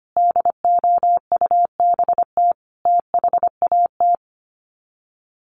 Morse code, heard as a keyed tone, about 25 wpm, 700 Hz. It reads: DOUBT THAT
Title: Sets of 5 Words 25wpm